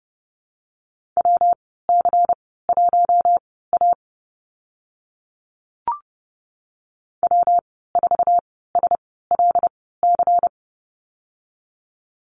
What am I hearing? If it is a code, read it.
Morse code, heard as a keyed tone, about 30 wpm, spaced as for 17 wpm, 700 Hz. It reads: WC1A W4SLC
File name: Lesson 19g - Letter w - Call Signs 30wpm